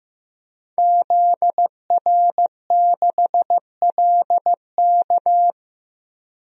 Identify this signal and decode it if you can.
Morse code, heard as a keyed tone, about 16 wpm, 700 Hz. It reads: ZR6LK
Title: DXCC-100 Call Signs - 5 Characters - 15wpm